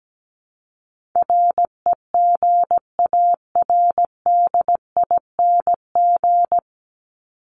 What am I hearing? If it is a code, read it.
Morse code, heard as a keyed tone, about 17 wpm, 700 Hz. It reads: REGARDING
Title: ING Suffix - Top 100 Words with Base Words 17wpm